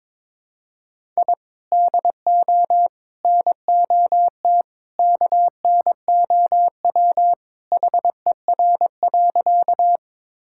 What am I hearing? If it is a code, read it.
Morse code, heard as a keyed tone, about 22 wpm, 700 Hz. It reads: I DO NOT KNOW HER.
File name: practice / 22wpm / Sentences with 2-Word Phrases - 22wpm